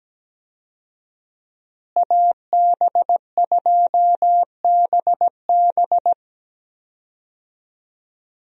Morse code, heard as a keyed tone, about 17 wpm, 700 Hz. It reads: AB2BB
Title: USA Call Signs - 5 Characters - Repeated 3x - 17wpm